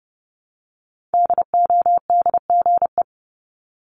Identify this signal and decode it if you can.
Morse code, heard as a keyed tone, about 30 wpm, 700 Hz. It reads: DODGE
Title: Multi-letter Phonemes Repeated 3x 30wpm